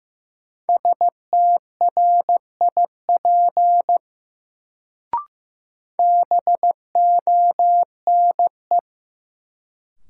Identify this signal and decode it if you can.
Morse code, heard as a keyed tone, about 15 wpm, 700 Hz. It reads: STRIP BONE
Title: Sentences from Top 1600 Words - New Word Drill - 15wpm